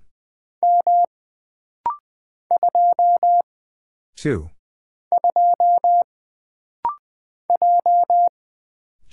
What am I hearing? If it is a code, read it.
Morse code, heard as a keyed tone, about 20 wpm, 700 Hz. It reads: M 2 2 J
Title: Single Character (Letter:Number:Punctuation:Prosign) - Rapid-Fire - 20wpm